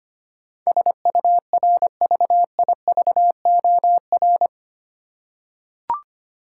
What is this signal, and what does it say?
Morse code, heard as a keyed tone, about 25 wpm, 700 Hz. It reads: SURVIVOR
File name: 8 Letter Words - First 3 Letters Given - 25wpm